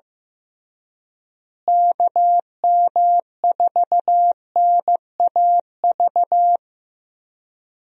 Morse code, heard as a keyed tone, about 15 wpm, 700 Hz. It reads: KM4NAV
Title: USA Call Signs - 6 Characters - Repeated 3x - 15wpm